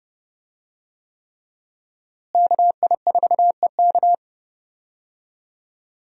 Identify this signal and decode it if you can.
Morse code, heard as a keyed tone, about 32 wpm, 700 Hz. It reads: KI4EK
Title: USA Call Signs - 5 Characters - Repeated 3x - 30wpm